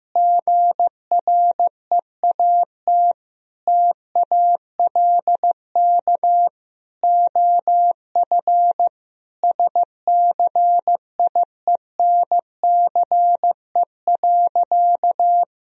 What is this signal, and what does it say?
Morse code, heard as a keyed tone, about 15 wpm, 700 Hz. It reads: GREAT TALK OF SCIENCE.
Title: Sentences from Top 1000 Words - Review Part 2 - 15wpm